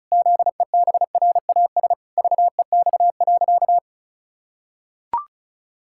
Morse code, heard as a keyed tone, about 36 wpm, 700 Hz. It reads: ZEBRAS VEX.
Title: Pangram Challenge v2 - 35wpm